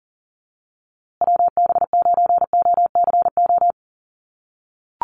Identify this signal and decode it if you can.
Morse code, heard as a keyed tone, about 40 wpm, 700 Hz. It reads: WB9OCO
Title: USA Call Signs Repeated 3x 40wpm